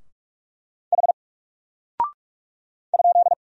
Morse code, heard as a keyed tone, about 44 wpm, 700 Hz. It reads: H ?
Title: Single Character (Letter:Number:Punctuation:Prosign) Alt 45wpm